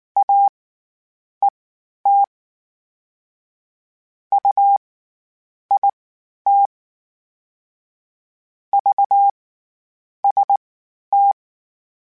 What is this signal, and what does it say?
Morse code, heard as a keyed tone, about 19 wpm, spaced as for 9 wpm, 800 Hz. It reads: AET UIT VST
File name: CW Comparisons / CW Comparisons-mp3s / CW Comparisons Broken Characters @18